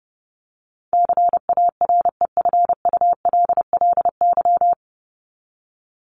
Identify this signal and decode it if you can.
Morse code, heard as a keyed tone, about 30 wpm, 700 Hz. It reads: CAREFULLY
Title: LY Suffix - Top 100 Words with Base Words 30wpm